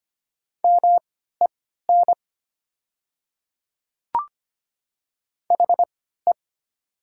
Morse code, heard as a keyed tone, about 25 wpm, spaced as for 14 wpm, 700 Hz. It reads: MEN HE
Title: 3x Character Spacing - Top 200 Words - 25wpm